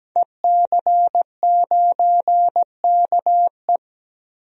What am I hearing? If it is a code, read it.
Morse code, heard as a keyed tone, about 17 wpm, 700 Hz. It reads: EC9KE
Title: DXCC-100 Call Signs - 5 Characters - 17wpm